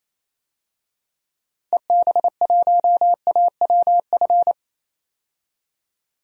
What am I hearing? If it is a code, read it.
Morse code, heard as a keyed tone, about 27 wpm, 700 Hz. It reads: EB1AWF
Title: DXCC-100 Call Signs - 6 Characters - 28wpm